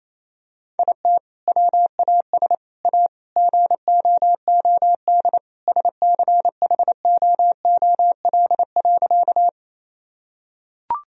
Morse code, heard as a keyed tone, about 28 wpm, 700 Hz. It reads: IT WAS A GOOD SCHOOL.
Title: Sentences from Top 1000 Words - Review Part 1 - 28wpm